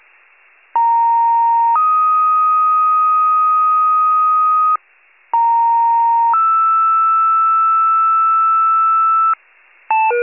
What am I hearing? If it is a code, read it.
Two-tone page, heard as a one-second tone, then a three-second tone, about 930 then 1250 Hz, about 930 then 1320 Hz. It is heard from 0.7 to 9.3 s.
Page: 82 FIRE